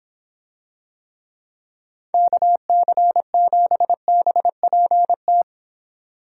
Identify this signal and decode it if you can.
Morse code, heard as a keyed tone, about 25 wpm, 700 Hz. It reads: KC7BPT